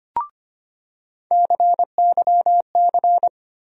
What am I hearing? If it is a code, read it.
Morse code, heard as a keyed tone, about 25 wpm, 700 Hz. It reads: CYC
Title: USA Call Sign Postfixes - 3 Letters 25wpm